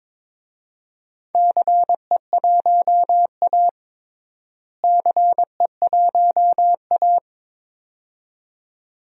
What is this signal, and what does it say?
Morse code, heard as a keyed tone, about 23 wpm, 700 Hz. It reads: CE1A CE1A
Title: DXCC-100 Call Signs - 4 Characters - Repeated 2x - 22wpm